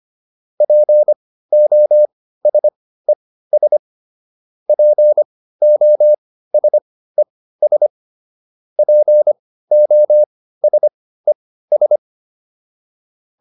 Morse code, heard as a keyed tone, about 25 wpm, spaced as for 15 wpm, 600 Hz. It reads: POSES POSES POSES